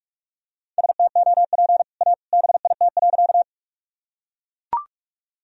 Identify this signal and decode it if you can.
Morse code, heard as a keyed tone, about 42 wpm, 700 Hz. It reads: STOP A BIT.
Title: Sentences from Top 1000 Words - Review Part 1 - 45wpm